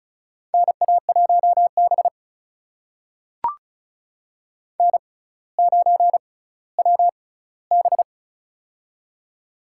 Morse code, heard as a keyed tone, about 36 wpm, spaced as for 18 wpm, 700 Hz. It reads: NA1B N9WB
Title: USA Call Signs - 4 Characters - 6x Character Spacing - 35wpm